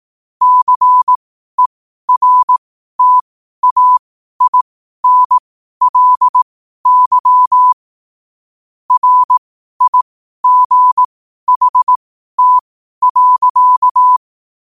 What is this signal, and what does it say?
Morse code, heard as a keyed tone, about 18 wpm, spaced as for 12 wpm, 1000 Hz. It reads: CERTAINLY RIGHT.
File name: qotdfiles / 12wpm